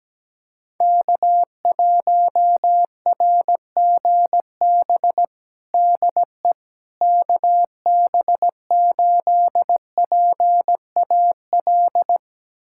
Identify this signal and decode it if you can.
Morse code, heard as a keyed tone, about 17 wpm, 700 Hz. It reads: K1RGB DE KB8PAL
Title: USA Callsign DE Callsign 17wpm